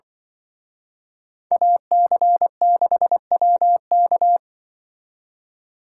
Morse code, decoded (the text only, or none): AC6WK